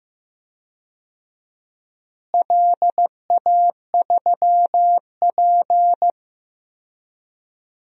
Morse code, heard as a keyed tone, about 15 wpm, 700 Hz. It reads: LA3P
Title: DXCC-100 Call Signs - 4 Characters - Repeated 3x - 15wpm